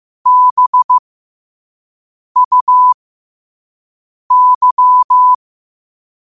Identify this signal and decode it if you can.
Morse code, heard as a keyed tone, about 15 wpm, spaced as for 5 wpm, 1000 Hz. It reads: BUY
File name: qotdfiles / 5wpm